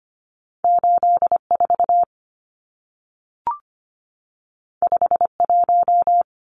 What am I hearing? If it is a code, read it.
Morse code, heard as a keyed tone, about 25 wpm, 700 Hz. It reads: 84 51
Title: Numbers - 2-Digit 25wpm